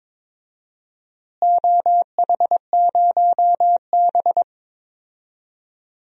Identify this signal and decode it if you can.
Morse code, heard as a keyed tone, about 22 wpm, 700 Hz. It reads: OH0B
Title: DXCC-100 Call Signs - 4 Characters - Repeated 3x - 22wpm